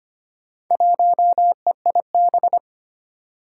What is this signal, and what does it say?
Morse code, heard as a keyed tone, about 25 wpm, 700 Hz. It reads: 1EIB
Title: Four Challenging Letter-Number Combinations 25wpm